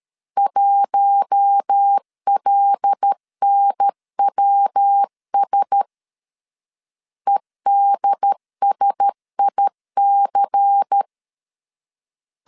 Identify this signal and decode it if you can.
Morse code, heard as a keyed tone, about 13 wpm, 800 Hz. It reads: 1LNWS EDSIC
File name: LEZIONI / 6 PKX0 / 6 PKX0 12wpm 2231